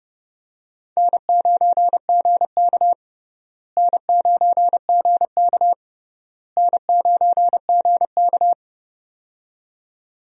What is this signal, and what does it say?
Morse code, heard as a keyed tone, about 30 wpm, 700 Hz. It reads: N9GK N9GK N9GK
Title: USA Call Signs - 4 Characters - Repeated 3x - 30wpm